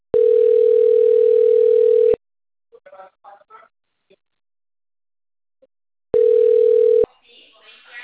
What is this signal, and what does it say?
Ringback tone (North America), 440 + 480 Hz.
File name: out- - - - - .96671